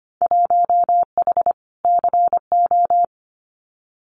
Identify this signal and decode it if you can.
Morse code, heard as a keyed tone, about 25 wpm, 700 Hz. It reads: 1H CO
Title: Winter Field Day Exchanges 25wpm